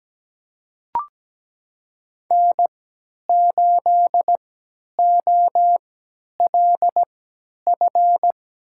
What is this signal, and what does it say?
Morse code, heard as a keyed tone, about 17 wpm, spaced as for 10 wpm, 700 Hz. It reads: N8OLF
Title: USA Call Signs - 5 Characters - 3x Character Spacing - 17wpm